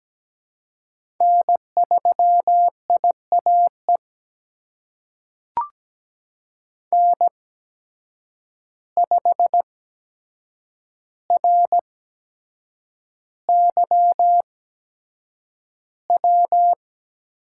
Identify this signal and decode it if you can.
Morse code, heard as a keyed tone, about 17 wpm, spaced as for 7 wpm, 700 Hz. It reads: N3IAE N5RYW
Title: USA Call Signs - 5 Characters - 8x Character Spacing - 17wpm